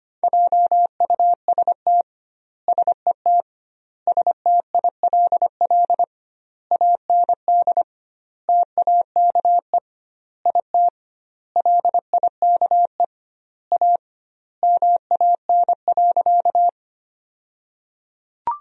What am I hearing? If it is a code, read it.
Morse code, heard as a keyed tone, about 25 wpm, 700 Hz. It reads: JUST SET STILL AND TAKE IT LIKE A MAN.
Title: 2x Word-Spacing - Sentences from Top 300 Words - 25wpm